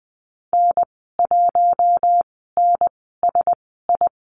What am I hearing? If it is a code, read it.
Morse code, heard as a keyed tone, about 20 wpm, spaced as for 14 wpm, 700 Hz. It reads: N1NSI
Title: 042 Lesson 8g - Number 1 - Call Signs 20wpm